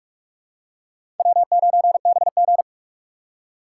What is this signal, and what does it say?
Morse code, heard as a keyed tone, about 43 wpm, 700 Hz. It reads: W9BG
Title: USA Call Signs - 4x Character Spacing - 45wpm